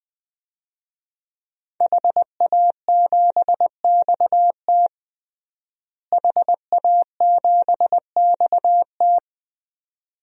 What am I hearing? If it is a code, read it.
Morse code, heard as a keyed tone, about 20 wpm, 700 Hz. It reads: HA7XT HA7XT
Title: DXCC-100 Call Signs - 5 Characters - Repeated 2x - 20wpm